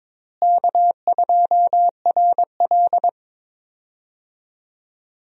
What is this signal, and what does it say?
Morse code, heard as a keyed tone, about 22 wpm, 700 Hz. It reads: K2RL
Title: USA Call Signs - 4 Characters - Repeated 2x - 22wpm